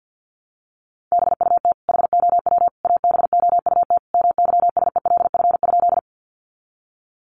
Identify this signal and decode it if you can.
Morse code, heard as a keyed tone, about 50 wpm, 700 Hz. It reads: BUT HOW ABOUT MYSELF?
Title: Sentences from Top 1300 Words 50wpm